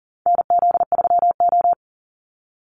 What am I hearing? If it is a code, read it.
Morse code, heard as a keyed tone, about 40 wpm, 700 Hz. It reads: NZ3O
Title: Call Signs Repeated 3x - 40wpm